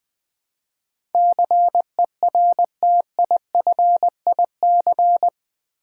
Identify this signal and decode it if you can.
Morse code, heard as a keyed tone, about 20 wpm, 700 Hz. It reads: CERTIFIC
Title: Build A Word - 4 Letters At A Time - 11-Letter Words - 20wpm